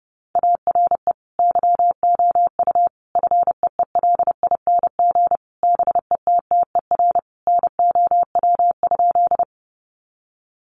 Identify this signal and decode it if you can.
Morse code, heard as a keyed tone, about 30 wpm, 700 Hz. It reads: ARE YOU FEELING BETTER NOW?
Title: Sentences from Top 1200 Words 30wpm